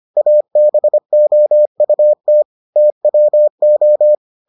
Morse code, heard as a keyed tone, about 25 wpm, 600 Hz. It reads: ABOUT TWO